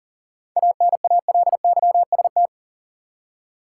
Morse code, heard as a keyed tone, about 40 wpm, 700 Hz. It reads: ANALYST
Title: Seven Letter Words - 40wpm